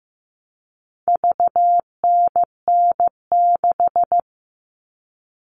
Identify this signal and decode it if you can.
Morse code, heard as a keyed tone, about 15 wpm, 700 Hz. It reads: VNN6
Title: All Cut Numbers - 4-Digit 15wpm